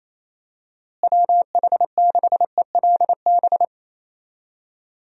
Morse code, heard as a keyed tone, about 28 wpm, 700 Hz. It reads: WH6ELB
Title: USA Call Signs - 6 Characters - Repeated 3x - 28wpm